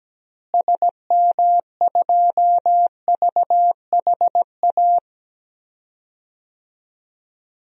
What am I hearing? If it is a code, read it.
Morse code, heard as a keyed tone, about 17 wpm, 700 Hz. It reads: SM2VHA
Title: DXCC-100 Call Signs - 6 Characters - Repeated 2x - 17wpm